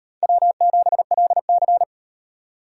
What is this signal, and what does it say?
Morse code, heard as a keyed tone, about 36 wpm, 700 Hz. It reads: W7LC